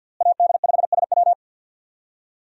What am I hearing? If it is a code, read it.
Morse code, heard as a keyed tone, about 49 wpm, 700 Hz. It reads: AD5SW
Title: USA Call Signs - 5 Characters - Repeated 3x - 50wpm